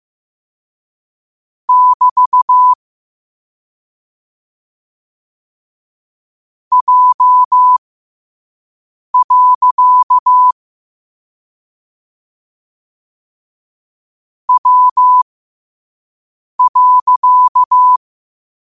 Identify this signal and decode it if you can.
Morse code, heard as a keyed tone, about 15 wpm, spaced as for 5 wpm, 1000 Hz. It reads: <BT> J. W.
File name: qotdfiles / 5wpm